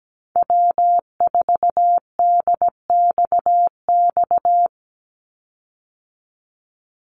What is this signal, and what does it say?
Morse code, heard as a keyed tone, about 17 wpm, 700 Hz. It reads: W4DXX
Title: Call Signs Repeated 2x - 17wpm